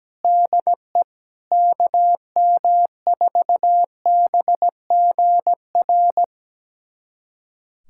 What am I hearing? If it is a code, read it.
Morse code, heard as a keyed tone, about 17 wpm, 700 Hz. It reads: DE KM4BGR
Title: USA Callsign DE Callsign 17wpm